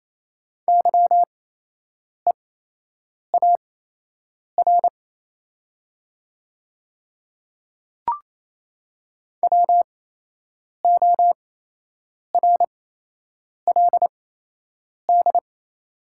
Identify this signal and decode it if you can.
Morse code, heard as a keyed tone, about 28 wpm, spaced as for 8 wpm, 700 Hz. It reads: YEAR WORLD
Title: 8x Character Spacing - Top 500 Words - 28wpm